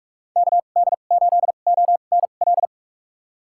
Morse code, heard as a keyed tone, about 43 wpm, spaced as for 32 wpm, 700 Hz. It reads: KD8ONL
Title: USA Call Signs - 6 Characters - 2x Character Spacing - 45wpm